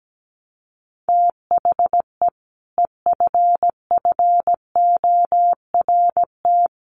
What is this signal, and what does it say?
Morse code, heard as a keyed tone, about 17 wpm, 700 Hz. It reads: THE EFFORT
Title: Sentences from Top 1200 Words 17wpm